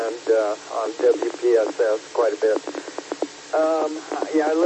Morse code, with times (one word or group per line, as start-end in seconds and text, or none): none